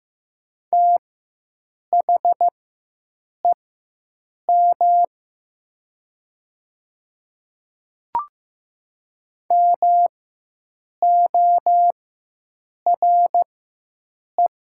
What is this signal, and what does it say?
Morse code, heard as a keyed tone, about 15 wpm, spaced as for 7 wpm, 700 Hz. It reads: THEM MORE
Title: 4x Character Spacing - Top 100 Words - 15wpm